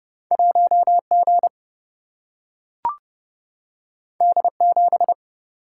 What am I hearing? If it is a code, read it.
Morse code, heard as a keyed tone, about 30 wpm, 700 Hz. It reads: 1 G D 7
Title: Single-Letter Single-Number 30wpm